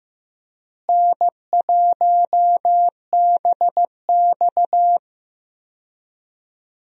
Morse code, heard as a keyed tone, about 15 wpm, 700 Hz. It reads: N1BX
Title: USA Call Signs - 4 Characters - Repeated 3x - 15wpm